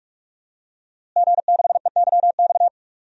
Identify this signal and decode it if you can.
Morse code, heard as a keyed tone, about 43 wpm, 700 Hz. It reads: G6EYX
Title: DXCC-100 Call Signs - 5 Characters - Repeated 3x - 45wpm